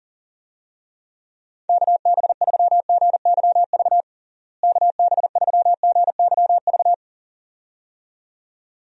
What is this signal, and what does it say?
Morse code, heard as a keyed tone, about 40 wpm, 700 Hz. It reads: KB3GYV KB3GYV
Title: USA Call Signs - 6 Characters - Repeated 2x - 40wpm